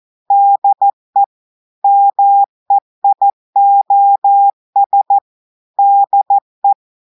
Morse code, heard as a keyed tone, about 14 wpm, 800 Hz. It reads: DE MEIOS DE
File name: audios / portugais / portugais texte en clair 2 vitesse 14 mots minutes